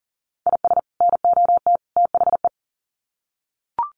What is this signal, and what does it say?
Morse code, heard as a keyed tone, about 40 wpm, 700 Hz. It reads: IS NOT THE
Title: Sets of 3 Words 40wpm